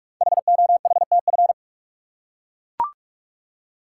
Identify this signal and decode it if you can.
Morse code, heard as a keyed tone, about 44 wpm, 690 Hz. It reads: HOHTF
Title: Code Groups - 5 Letters - 45wpm